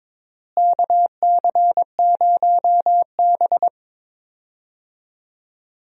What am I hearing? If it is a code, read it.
Morse code, heard as a keyed tone, about 22 wpm, 700 Hz. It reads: KC0B
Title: USA Call Signs - 4 Characters - Repeated 2x - 22wpm